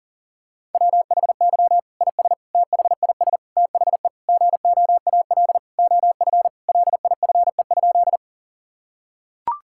Morse code, heard as a keyed tone, about 40 wpm, 700 Hz. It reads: WHY IS THIS THE GOAL OF LIFE?
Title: Sentences from Top 1000 Words - Review Part 5 - 40wpm